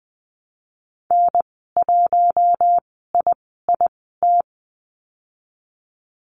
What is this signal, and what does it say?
Morse code, heard as a keyed tone, about 20 wpm, spaced as for 15 wpm, 700 Hz. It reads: N1IIT
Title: 048 Lesson 9g - Number 4 - Call Signs 20wpm